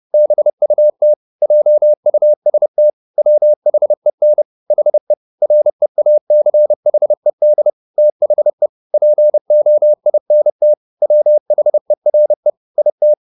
Morse code, heard as a keyed tone, about 30 wpm, 600 Hz. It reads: BUT JUST WHEN HE REACHED THE POINT WHERE IT